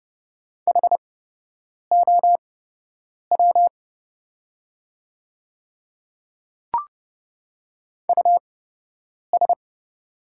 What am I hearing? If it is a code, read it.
Morse code, heard as a keyed tone, about 30 wpm, spaced as for 8 wpm, 690 Hz. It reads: HOW US
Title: 8x Character Spacing - Top 200 Words - 30wpm